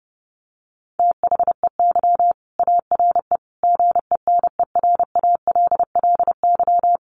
Morse code, heard as a keyed tone, about 30 wpm, 700 Hz. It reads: THEY ARE GENERALLY
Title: Sentences from Top 600 Words 30wpm